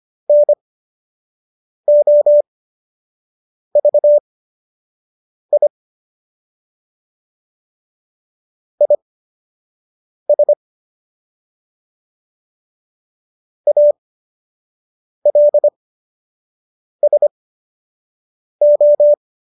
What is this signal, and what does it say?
Morse code, heard as a keyed tone, about 25 wpm, spaced as for 6 wpm, 590 Hz. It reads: NOVI IS ALSO